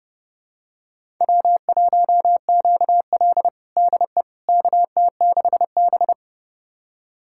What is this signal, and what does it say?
Morse code, heard as a keyed tone, about 30 wpm, 700 Hz. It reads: W1QL DE KT6B
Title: USA Callsign DE Callsign - 4 Characters - 30wpm